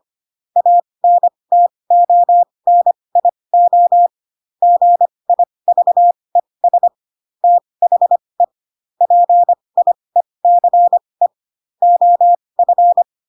Morse code, heard as a keyed tone, about 25 wpm, spaced as for 20 wpm, 700 Hz. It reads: ANTONIO GIVES THE PIECE OF